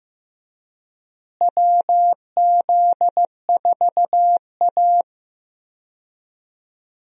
Morse code, heard as a keyed tone, about 15 wpm, 700 Hz. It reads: WZ4A
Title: USA Call Signs - 4 Characters - Repeated 3x - 15wpm